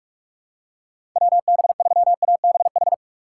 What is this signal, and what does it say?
Morse code, heard as a keyed tone, about 46 wpm, 690 Hz. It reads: WB3ABH
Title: USA Call Signs - 4x Character Spacing - 45wpm